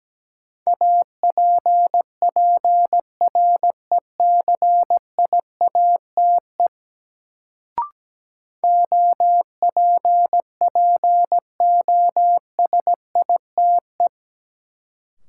Morse code, heard as a keyed tone, about 17 wpm, 700 Hz. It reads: APPRECIATE OPPOSITE
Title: Sentences from Top 1300 Words - New Word Drill - 17wpm